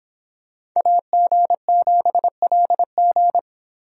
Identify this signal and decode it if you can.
Morse code, heard as a keyed tone, about 26 wpm, 700 Hz. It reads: AG7LG